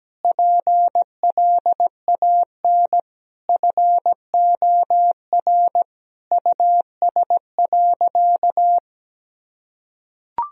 Morse code, heard as a keyed tone, about 17 wpm, 700 Hz. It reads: PLAN FOR US.
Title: Sentences from Top 1000 Words - Review Part 2 - 17wpm